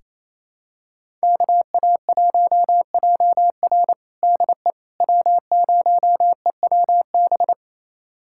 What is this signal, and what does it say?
Morse code, heard as a keyed tone, about 28 wpm, 700 Hz. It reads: KA1JR DE W0EWB
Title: USA Callsign DE Callsign - 5 Characters - 28wpm